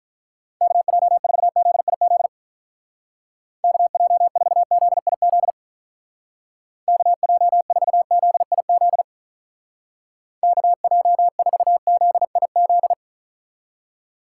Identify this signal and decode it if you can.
Morse code, heard as a keyed tone, about 44 wpm, 700 Hz. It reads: KJ4ZIZ KJ4ZIZ KJ4ZIZ KJ4ZIZ